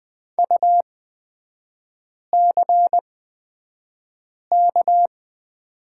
Morse code, heard as a keyed tone, about 20 wpm, spaced as for 5 wpm, 700 Hz. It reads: UCK
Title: 224 Lesson 36e - BT - Three Characters 20wpm